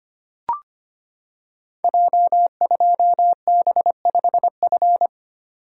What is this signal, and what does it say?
Morse code, heard as a keyed tone, about 25 wpm, 700 Hz. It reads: J2B5F
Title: Code Groups - 5 Letter:Numbers - 25wpm